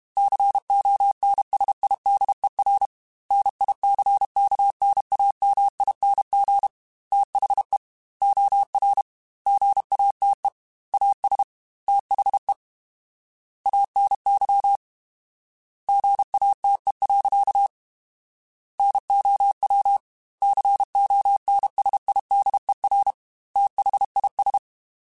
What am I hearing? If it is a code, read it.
Morse code, heard as a keyed tone, about 32 wpm, 800 Hz. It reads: CONSIDER NICKNAMING THE OR GATE AS THE ANY GATE. NOW CONSIDER THIS
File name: Technical Subjects / Technical Subjects-mp3s / OR Gate @30